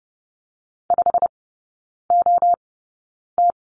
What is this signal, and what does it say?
Morse code, heard as a keyed tone, about 30 wpm, spaced as for 9 wpm, 700 Hz. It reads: 5OT
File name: Lesson 19e - Letter w - Three Characters 30wpm